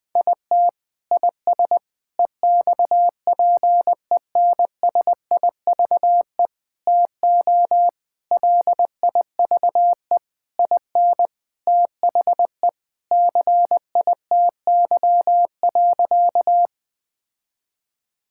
Morse code, heard as a keyed tone, about 20 wpm, 700 Hz. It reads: IT IS EXPENSIVE TO LIVE IN THE CITY.
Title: Sentences from Top 1000 Words - Review Part 6 - 20wpm